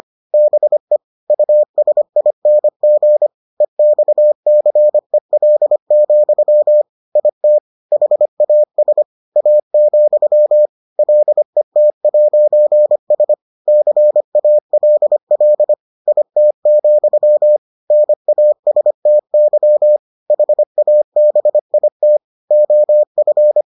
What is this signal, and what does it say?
Morse code, heard as a keyed tone, about 25 wpm, 600 Hz. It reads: BE USING EXCEL, IT HAS A, LET'S CALL IT, NASTY HABIT OF